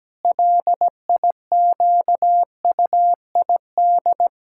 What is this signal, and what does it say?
Morse code, heard as a keyed tone, about 17 wpm, 700 Hz. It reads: LIQUID
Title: Pangram Challenge v2 - 17wpm